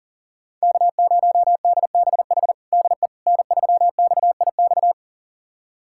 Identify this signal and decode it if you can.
Morse code, heard as a keyed tone, about 40 wpm, 690 Hz. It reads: K0DBH DE N3XIX
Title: USA Callsign DE Callsign - 5 Characters - 40wpm